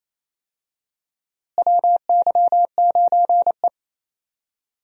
Morse code, heard as a keyed tone, about 27 wpm, 700 Hz. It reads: WY9E
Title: USA Call Signs - 4 Characters - Repeated 2x - 28wpm